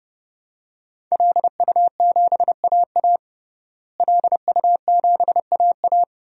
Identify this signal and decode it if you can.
Morse code, heard as a keyed tone, about 28 wpm, 700 Hz. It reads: LU7AA LU7AA
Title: DXCC-100 Call Signs - 5 Characters - Repeated 3x - 30wpm